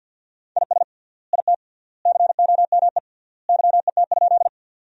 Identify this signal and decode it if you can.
Morse code, heard as a keyed tone, about 49 wpm, 700 Hz. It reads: IS IT COME YET?
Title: 3x Word-Spacing - Sentences from Top 500 Words - 50wpm